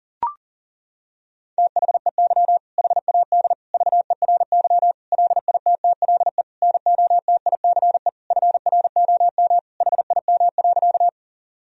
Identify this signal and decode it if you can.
Morse code, heard as a keyed tone, about 37 wpm, 700 Hz. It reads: THEY HAD VERY LITTLE NOTICE FROM HIM.
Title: Sentences from Top 1000 Words - Review Part 2 - 40wpm